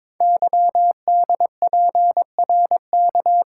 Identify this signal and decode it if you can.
Morse code, heard as a keyed tone, about 22 wpm, 700 Hz. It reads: YDPRK
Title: Code Groups - 5 Letters - 22wpm